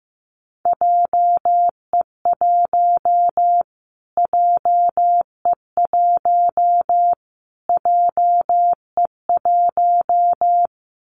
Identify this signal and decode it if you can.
Morse code, heard as a keyed tone, about 15 wpm, 700 Hz. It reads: JE1 JE1 JE1
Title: Callsign Prefixes for Top 25 DXCC - Repeated 3x - 15wpm